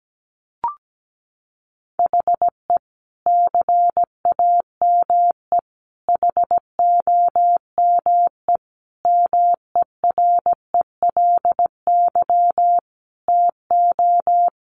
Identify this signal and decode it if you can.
Morse code, heard as a keyed tone, about 17 wpm, 700 Hz. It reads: HE CAME HOME MERELY TO
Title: Sentences from Top 1400 Words 17wpm